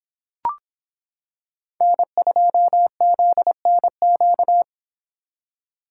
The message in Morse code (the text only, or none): N2ZNQ